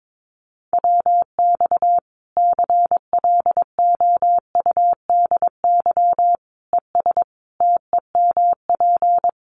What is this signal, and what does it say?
Morse code, heard as a keyed tone, about 22 wpm, 700 Hz. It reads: WX CLOUDY ES TEMP
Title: Common QSO Phrases 22wpm